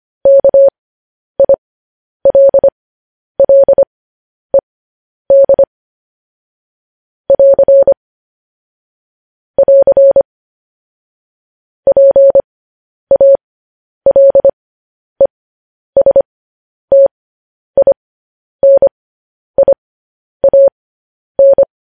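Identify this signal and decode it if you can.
Morse code, heard as a keyed tone, about 25 wpm, spaced as for 10 wpm, 550 Hz. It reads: KILLED <AR> <AR> PALESTINIAN
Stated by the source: Morse code practice sending news headlines at 10 wpm.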